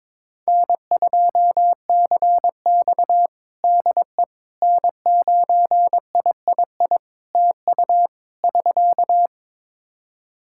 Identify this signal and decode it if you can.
Morse code, heard as a keyed tone, about 22 wpm, 700 Hz. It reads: N2CX DE N9III TU <SK>
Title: K1USN SST QSO Practice 22wpm